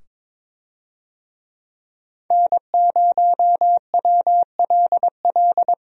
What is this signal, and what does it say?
Morse code, heard as a keyed tone, about 22 wpm, 700 Hz. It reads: N0WLL
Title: USA Call Signs - 5 Characters - 22wpm